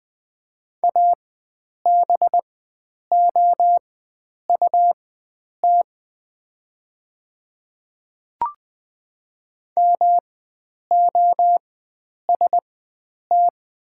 Morse code, heard as a keyed tone, about 20 wpm, spaced as for 9 wpm, 700 Hz. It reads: ABOUT MOST
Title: 4x Character Spacing - Top 100 Words - 20wpm